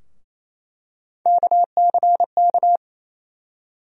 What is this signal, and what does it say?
Morse code, heard as a keyed tone, about 28 wpm, 700 Hz. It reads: KCK
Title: USA Call Sign Postfixes - 3 Letters 28wpm